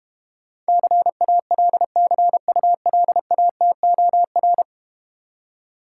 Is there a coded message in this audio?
Yes: Morse code from 0.7 to 4.6 s, heard as a keyed tone, about 32 wpm, 700 Hz. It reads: CALCULATOR